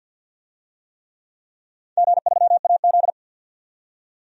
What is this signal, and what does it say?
Morse code, heard as a keyed tone, about 52 wpm, 700 Hz. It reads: G3AZ
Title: DXCC-100 Call Signs - 4 Characters - 50wpm